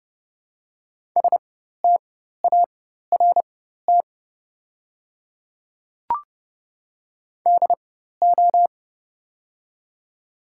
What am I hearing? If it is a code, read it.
Morse code, heard as a keyed tone, about 31 wpm, spaced as for 14 wpm, 700 Hz. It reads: START DO
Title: 4x Character Spacing - Top 500 Words - 30wpm